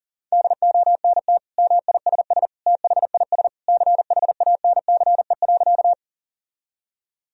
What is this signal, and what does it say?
Morse code, heard as a keyed tone, about 40 wpm, 700 Hz. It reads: DONT MISS THIS CHANCE.
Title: Sentences from Top 1000 Words - Review Part 3 - 40wpm